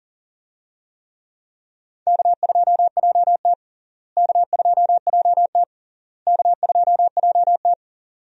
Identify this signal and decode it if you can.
Morse code, heard as a keyed tone, about 37 wpm, 700 Hz. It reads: K2JT K2JT K2JT
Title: USA Call Signs - 4 Characters - Repeated 3x - 40wpm